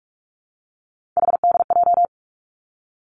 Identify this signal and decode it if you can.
Morse code, heard as a keyed tone, about 46 wpm, 700 Hz. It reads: HDJ
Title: Three Challenging Letter-Number Combinations 45wpm